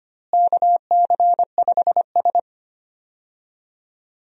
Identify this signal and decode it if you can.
Morse code, heard as a keyed tone, about 25 wpm, 700 Hz. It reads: KC5S
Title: USA Call Signs - 4 Characters - Repeated 2x - 25wpm